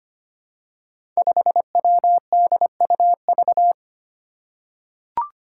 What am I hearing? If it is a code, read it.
Morse code, heard as a keyed tone, about 25 wpm, 700 Hz. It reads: 5WDUV
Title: Five Challenging Letter-Number Combinations 25wpm